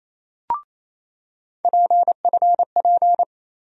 Morse code, heard as a keyed tone, about 27 wpm, 700 Hz. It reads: PFP
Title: USA Call Sign Postfixes - 3 Letters 28wpm